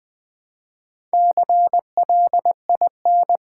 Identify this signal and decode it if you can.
Morse code, heard as a keyed tone, about 20 wpm, 700 Hz. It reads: CLIN
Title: Build A Word - 2 Letters At A Time - 8-Letter Words - 20wpm